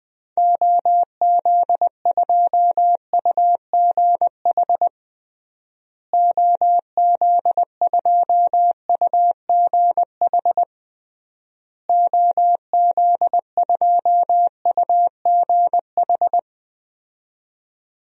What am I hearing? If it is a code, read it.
Morse code, heard as a keyed tone, about 20 wpm, 700 Hz. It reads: OZ2UGH OZ2UGH OZ2UGH
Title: DXCC-100 Call Signs - 6 Characters - Repeated 3x - 20wpm